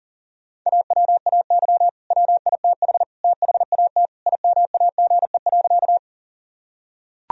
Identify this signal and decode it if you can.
Morse code, heard as a keyed tone, about 40 wpm, 700 Hz. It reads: AWAY WITH THAT IMAGE.
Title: Sentences from Top 1000 Words - Review Part 4 - 40wpm